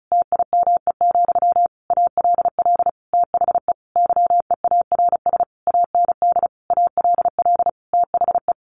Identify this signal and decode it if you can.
Morse code, heard as a keyed tone, about 34 wpm, 700 Hz. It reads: TIME, ALL THE YEARS AND ALL THE
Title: A Sound Of Thunder 35wpm